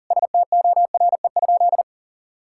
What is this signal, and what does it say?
Morse code, heard as a keyed tone, about 40 wpm, 700 Hz. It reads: STORE?
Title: Sentences from Top 1000 Words - Review Part 5 - 40wpm